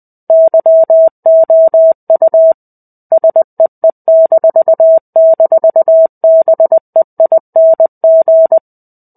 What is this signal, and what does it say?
Morse code, heard as a keyed tone, about 20 wpm, 650 Hz. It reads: YOU SEE--BEING
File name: 05-Chapter-05 20 WPM